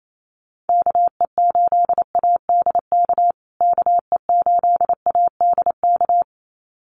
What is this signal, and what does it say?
Morse code, heard as a keyed tone, about 28 wpm, 700 Hz. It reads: KE8ADK KE8ADK
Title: USA Call Signs Repeated 2x 28wpm